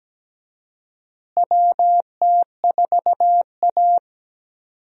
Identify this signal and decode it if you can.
Morse code, heard as a keyed tone, about 17 wpm, 700 Hz. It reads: WT4A
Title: USA Call Signs - 4 Characters - Repeated 3x - 17wpm